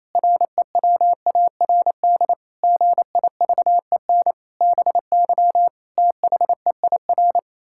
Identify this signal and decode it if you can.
Morse code, heard as a keyed tone, about 28 wpm, 700 Hz. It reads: REWARD GIVEN BY THEIR
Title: Pangram Challenge v2 - 28wpm